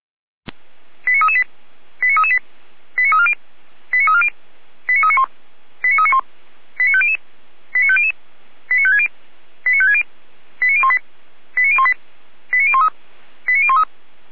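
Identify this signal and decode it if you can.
Five-tone selective call (ZVEI), heard as quick runs of five tones, tones of about 70 ms. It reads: 89338 89338 89340 89340 89391 89391 89500 89500 89560 89560 89018 89018 89013 89013
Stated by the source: ZVEI five-tone selective call received on SDR.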